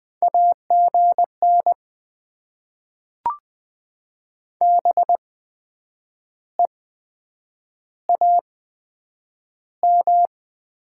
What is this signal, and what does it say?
Morse code, heard as a keyed tone, about 20 wpm, spaced as for 8 wpm, 700 Hz. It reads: AGN BEAM
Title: Top 90 QSO Elements - 8x Character Spacing - 20wpm